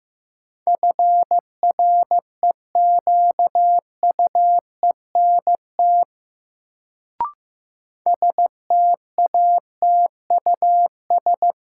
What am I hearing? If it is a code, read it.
Morse code, heard as a keyed tone, about 15 wpm, 700 Hz. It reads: FREQUENT STATUS
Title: Sentences from Top 1400 Words - New Word Drill - 15wpm